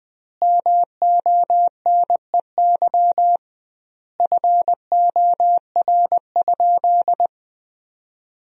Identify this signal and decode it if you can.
Morse code, heard as a keyed tone, about 21 wpm, 700 Hz. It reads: MONEY FOR?
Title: 2x Word-Spacing - Sentences from Top 400 Words - 20wpm